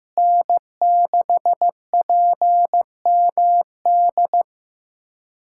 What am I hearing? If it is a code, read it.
Morse code, heard as a keyed tone, about 16 wpm, 700 Hz. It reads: N6PMD
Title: USA Callsign DE Callsign - 5 Characters - 15wpm